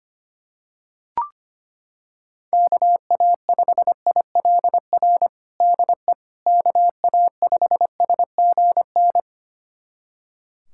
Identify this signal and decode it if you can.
Morse code, heard as a keyed tone, about 25 wpm, 700 Hz. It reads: KA5ILR DE KA5SGN
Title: USA Callsign DE Callsign - 6 Characters - 25wpm